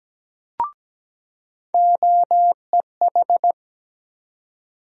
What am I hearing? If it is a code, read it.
Morse code, heard as a keyed tone, about 17 wpm, 700 Hz. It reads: OEH
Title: USA Call Sign Postfixes - 3 Letters 17wpm